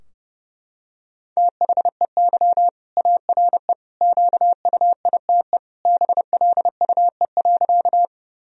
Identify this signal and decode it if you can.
Morse code, heard as a keyed tone, about 32 wpm, 700 Hz. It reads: THEY ARE QUITE BLUE.
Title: Sentences from Top 1000 Words - Review Part 3 - 30wpm